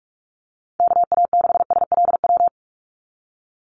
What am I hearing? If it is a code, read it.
Morse code, heard as a keyed tone, about 46 wpm, 700 Hz. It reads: KA6SLW
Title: USA Call Signs 45wpm